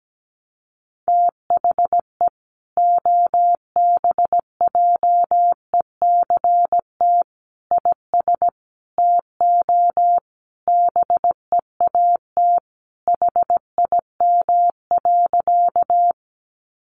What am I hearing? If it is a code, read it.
Morse code, heard as a keyed tone, about 17 wpm, 700 Hz. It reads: THE OBJECT IS TO BEAT HIM.
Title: Sentences from Top 1300 Words 17wpm